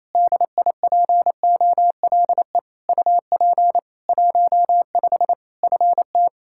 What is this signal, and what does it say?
Morse code, heard as a keyed tone, about 28 wpm, 700 Hz. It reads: DIPOLE UP 15 FT
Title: Practice QSO - Exchange 2 - Info - 28wpm